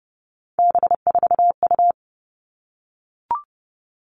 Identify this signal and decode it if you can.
Morse code, heard as a keyed tone, about 28 wpm, 700 Hz. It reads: B4U
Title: All Cut Numbers - 3-Digit 30wpm